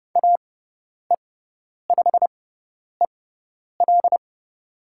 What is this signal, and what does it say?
Morse code, heard as a keyed tone, about 30 wpm, spaced as for 10 wpm, 700 Hz. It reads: AE5EL
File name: Lesson 16g - Letter c - Call Signs 30wpm